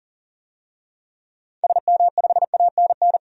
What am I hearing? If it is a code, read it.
Morse code, heard as a keyed tone, about 40 wpm, 700 Hz. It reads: SM5ANN
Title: DXCC-100 Call Signs - 6 Characters - 40wpm